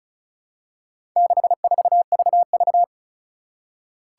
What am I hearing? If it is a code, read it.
Morse code, heard as a keyed tone, about 36 wpm, 700 Hz. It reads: 64VV
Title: Four Challenging Letter-Number Combinations 35wpm